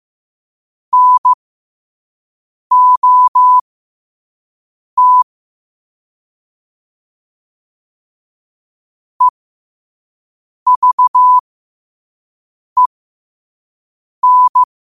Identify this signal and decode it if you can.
Morse code, heard as a keyed tone, about 15 wpm, spaced as for 5 wpm, 1000 Hz. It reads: NOT EVEN
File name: qotdfiles / 5wpm